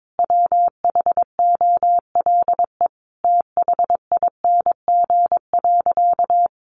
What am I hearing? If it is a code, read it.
Morse code, heard as a keyed tone, about 22 wpm, 700 Hz. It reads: WHOLE THING.
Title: Sentences from Top 400 Words 22wpm